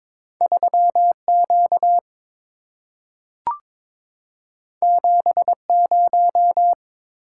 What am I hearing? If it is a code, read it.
Morse code, heard as a keyed tone, about 22 wpm, 690 Hz. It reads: 3Q 70
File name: practice / 22wpm / Two Letter-Number Combinations - 22wpm